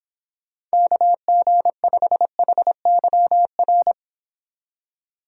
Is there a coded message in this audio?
Yes: Morse code from 0.7 to 3.9 s, heard as a keyed tone, about 25 wpm, 700 Hz. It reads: KG5HYR